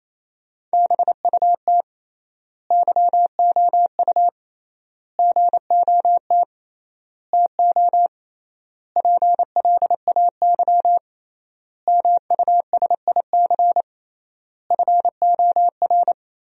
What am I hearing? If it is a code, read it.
Morse code, heard as a keyed tone, about 28 wpm, 700 Hz. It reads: BUT YOU GOT TO PLAY MUSIC FOR
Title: 3x Word-Spacing - Sentences from Top 300 Words - 28wpm